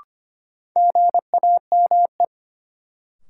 Morse code, heard as a keyed tone, about 25 wpm, 700 Hz. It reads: GAME
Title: Sentences from Top 500 Words - New Word Drill - 25wpm